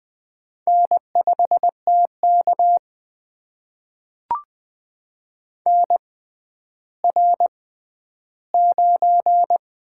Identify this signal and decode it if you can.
Morse code, heard as a keyed tone, about 20 wpm, spaced as for 11 wpm, 700 Hz. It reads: N5TK NR9
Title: USA Call Signs - 4 Characters - 6x Character Spacing - 20wpm